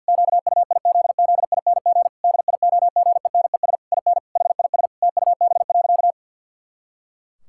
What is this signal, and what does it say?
Morse code, heard as a keyed tone, about 51 wpm, 700 Hz. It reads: QUIZZING DIOGENES IN HIS TUB.
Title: Pangram Challenge v2 - 50wpm